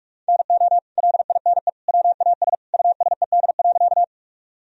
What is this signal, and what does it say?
Morse code, heard as a keyed tone, about 47 wpm, 700 Hz. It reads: NO LINE WAS USED.
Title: Sentences from Top 1000 Words - Review Part 1 - 45wpm